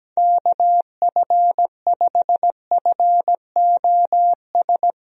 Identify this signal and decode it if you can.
Morse code, heard as a keyed tone, about 17 wpm, 700 Hz. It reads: KF5FOS
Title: USA Call Signs - 6 Characters - Repeated 3x - 17wpm